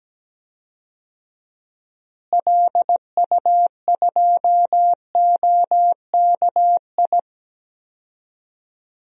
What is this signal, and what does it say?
Morse code, heard as a keyed tone, about 17 wpm, 700 Hz. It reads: LU2OKI
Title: DXCC-100 Call Signs - 6 Characters - Repeated 3x - 17wpm